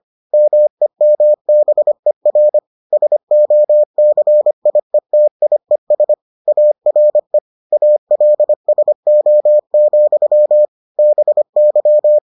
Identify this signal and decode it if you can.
Morse code, heard as a keyed tone, about 25 wpm, 600 Hz. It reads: MEMBER SOCIETIES ARE ALSO, BY